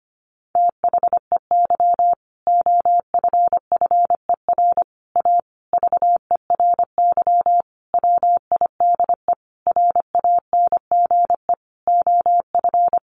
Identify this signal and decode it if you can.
Morse code, heard as a keyed tone, about 25 wpm, 700 Hz. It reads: THEY OFFER A VERY WIDE RANGE OF
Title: Sentences from Top 900 Words 25wpm